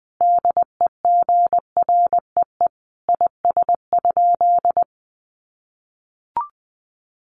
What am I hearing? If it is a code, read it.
Morse code, heard as a keyed tone, about 20 wpm, 700 Hz. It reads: DEGREE IS?
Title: Sentences from Top 1500 Words 20wpm